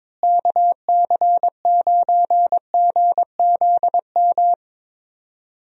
Morse code, heard as a keyed tone, about 22 wpm, 700 Hz. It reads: KC9GZM
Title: USA Call Signs - 6 Characters - Repeated 3x - 22wpm